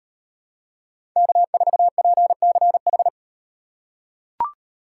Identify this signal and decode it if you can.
Morse code, heard as a keyed tone, about 36 wpm, 700 Hz. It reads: K4PCH